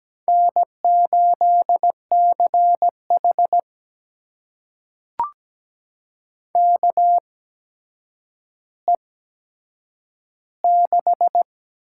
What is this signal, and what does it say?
Morse code, heard as a keyed tone, about 17 wpm, spaced as for 8 wpm, 700 Hz. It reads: N8CH KE6
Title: USA Call Signs - 8x Character Spacing - 17wpm